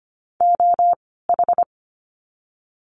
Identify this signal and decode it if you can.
Morse code, heard as a keyed tone, about 25 wpm, 700 Hz. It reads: O H
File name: course / 25:10wpm / 094 Lesson 16f - Letter c - Words 25wpm